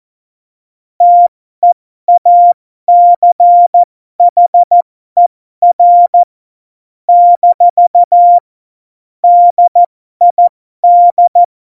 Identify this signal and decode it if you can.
Morse code, heard as a keyed tone, about 14 wpm, spaced as for 12 wpm, 700 Hz. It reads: TEACHER - DID
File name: Morsekurs / Witze